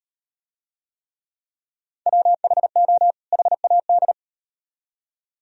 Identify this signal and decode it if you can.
Morse code, heard as a keyed tone, about 34 wpm, 700 Hz. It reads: WHO HAD